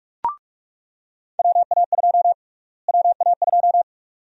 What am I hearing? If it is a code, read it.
Morse code, heard as a keyed tone, about 46 wpm, 700 Hz. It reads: WA2 WA2
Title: USA Call Sign Prefixes - Repeated 2x - 45wpm